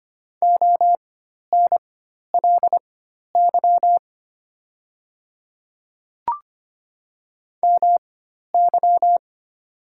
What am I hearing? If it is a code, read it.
Morse code, heard as a keyed tone, about 25 wpm, spaced as for 12 wpm, 690 Hz. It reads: ONLY MY
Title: 4x Character Spacing - Top 200 Words - 25wpm